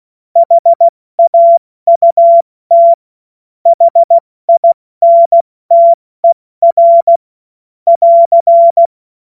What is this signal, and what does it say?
Morse code, heard as a keyed tone, about 16 wpm, 680 Hz. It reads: HAUT HINTER <AR>
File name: Morsekurs / Witze / Witz 09 April 2024 70 F80